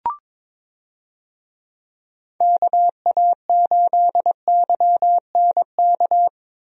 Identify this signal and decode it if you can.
Morse code, heard as a keyed tone, about 22 wpm, 700 Hz. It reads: KA8YNK
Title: USA Call Signs - 6 Characters - Repeated 3x - 22wpm